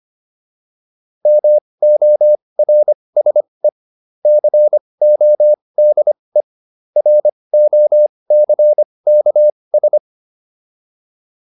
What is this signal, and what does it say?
Morse code, heard as a keyed tone, about 25 wpm, spaced as for 20 wpm, 600 Hz. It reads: MORSE CODE ROCKS